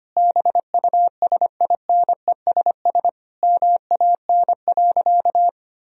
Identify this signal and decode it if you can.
Morse code, heard as a keyed tone, about 25 wpm, 700 Hz. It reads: BUSINESS MAN.
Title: Sentences from Top 1000 Words - Review Part 4 - 25wpm